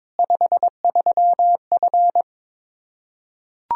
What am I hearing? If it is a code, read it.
Morse code, heard as a keyed tone, about 22 wpm, 700 Hz. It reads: 53F
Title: Practice QSO - Exchange 2 - Info - 22wpm